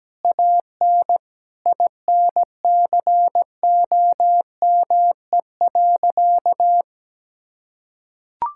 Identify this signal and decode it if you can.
Morse code, heard as a keyed tone, about 17 wpm, 700 Hz. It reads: AN INCOME.
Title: Sentences from Top 1000 Words - Review Part 5 - 17wpm